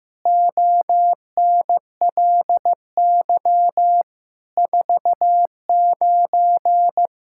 Morse code, heard as a keyed tone, about 16 wpm, 700 Hz. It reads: ONLY 49
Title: Pangram Challenge v2 - 15wpm